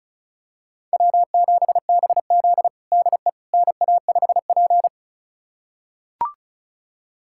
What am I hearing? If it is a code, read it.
Morse code, heard as a keyed tone, about 36 wpm, 700 Hz. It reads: W7BZ DE NA5P
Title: USA Callsign DE Callsign - 4 Characters - 35wpm